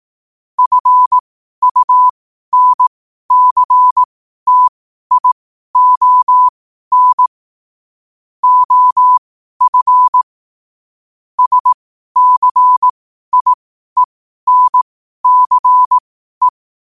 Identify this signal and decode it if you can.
Morse code, heard as a keyed tone, about 18 wpm, spaced as for 12 wpm, 1000 Hz. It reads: FUNCTION OF SCIENCE
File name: qotdfiles / 12wpm